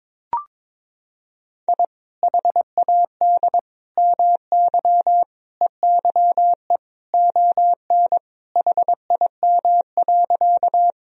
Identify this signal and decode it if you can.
Morse code, heard as a keyed tone, about 22 wpm, 700 Hz. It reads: I HAD MY EYE ON HIM.
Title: Sentences from Top 1000 Words - Review Part 1 - 22wpm